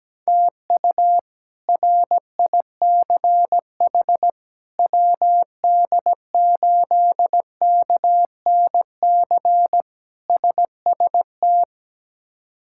Morse code, heard as a keyed tone, about 17 wpm, 700 Hz. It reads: TU RICH WD8KNC SST
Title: K1USN SST QSO Practice - 17wpm